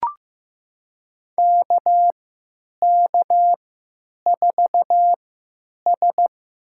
Morse code, heard as a keyed tone, about 15 wpm, spaced as for 9 wpm, 700 Hz. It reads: KK4S
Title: USA Call Signs - 3x Character Spacing - 15wpm